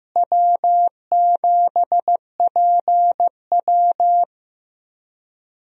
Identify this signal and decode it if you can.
Morse code, heard as a keyed tone, about 15 wpm, 700 Hz. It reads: W7PW
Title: USA Call Signs - 4 Characters - Repeated 3x - 15wpm